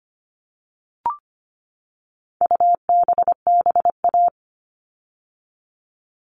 Morse code, heard as a keyed tone, about 25 wpm, 700 Hz. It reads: UBBA
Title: All Cut Numbers - 4-Digit 25wpm